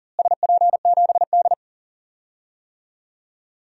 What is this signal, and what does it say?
Morse code, heard as a keyed tone, about 37 wpm, 700 Hz. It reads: SP7D
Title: DXCC-100 Call Signs - 4 Characters - Repeated 2x - 40wpm